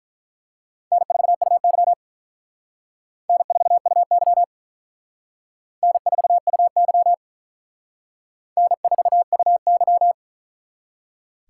Morse code, heard as a keyed tone, about 43 wpm, 700 Hz. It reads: N4UY N4UY N4UY N4UY
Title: Speed Racing - USA 1x2 Call Signs 35wpm